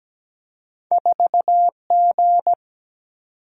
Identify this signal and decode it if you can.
Morse code, heard as a keyed tone, about 17 wpm, 700 Hz. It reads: 4G
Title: Single-Letter Single-Number 17wpm